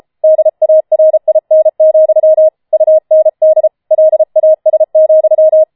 Morse code, heard as a keyed tone, about 33 wpm, 630 Hz. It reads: DARIN, UND LAS,
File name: Schneewittchen Teil5 34wpm